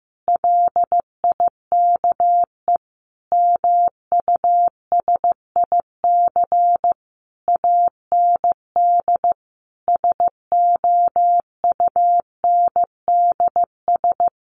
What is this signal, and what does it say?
Morse code, heard as a keyed tone, about 14 wpm, 700 Hz. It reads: LIKE MUSIC AND SOUNDS
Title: A Sound Of Thunder 15wpm